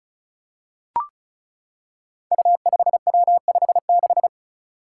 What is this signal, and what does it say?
Morse code, heard as a keyed tone, about 34 wpm, 700 Hz. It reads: U5W56
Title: Five Challenging Letter-Number Combinations 35wpm